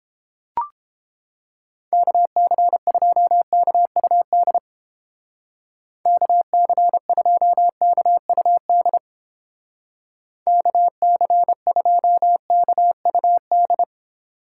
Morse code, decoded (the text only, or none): KC2KUD KC2KUD KC2KUD